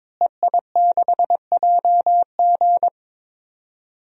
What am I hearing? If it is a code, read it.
Morse code, heard as a keyed tone, about 22 wpm, 700 Hz. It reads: EI6JG
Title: DXCC-100 Call Signs - 5 Characters - Repeated 3x - 22wpm